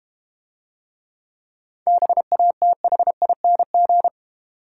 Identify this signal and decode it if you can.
Morse code, heard as a keyed tone, about 31 wpm, 700 Hz. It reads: BATHING
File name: Easy Words - Speed-Racing with Hints - 25wpm